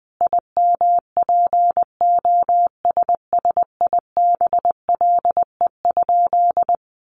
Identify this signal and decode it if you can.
Morse code, heard as a keyed tone, about 20 wpm, 700 Hz. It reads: IMPOSSIBLE?
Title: Sentences from Top 1100 Words 20wpm